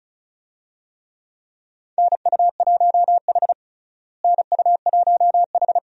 Morse code, heard as a keyed tone, about 36 wpm, 700 Hz. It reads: NU1H NU1H
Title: USA Call Signs - 4 Characters - Repeated 3x - 35wpm